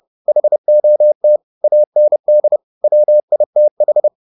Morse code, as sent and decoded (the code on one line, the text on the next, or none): .... --- - / .- -. -.. / .-- .. - ....
HOT AND WITH